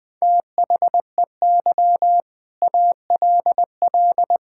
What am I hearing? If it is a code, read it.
Morse code, heard as a keyed tone, about 20 wpm, 700 Hz. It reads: THEY ALL
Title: Sentences from Top 1600 Words 20wpm